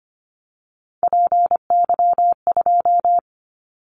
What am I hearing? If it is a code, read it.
Morse code, heard as a keyed tone, about 25 wpm, 700 Hz. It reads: PY2
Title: Callsign Prefixes for Top 50 DXCC - 25wpm